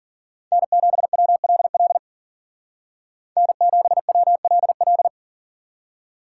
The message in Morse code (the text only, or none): N7WLL N7WLL